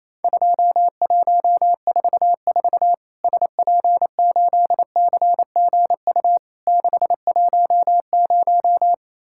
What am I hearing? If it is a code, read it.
Morse code, heard as a keyed tone, about 28 wpm, 700 Hz. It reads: 2144 SP8CGU 610
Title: Worked All Europe DX Contest - QTC Exchange - 28wpm